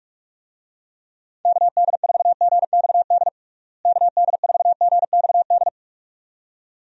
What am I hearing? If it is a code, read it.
Morse code, heard as a keyed tone, about 46 wpm, 700 Hz. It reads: KD4GXD KD4GXD
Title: USA Call Signs - 6 Characters - Repeated 2x - 45wpm